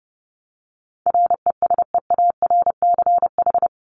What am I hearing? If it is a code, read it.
Morse code, heard as a keyed tone, about 30 wpm, 700 Hz. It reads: RESEARCH
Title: Top 501-750 Words 30wpm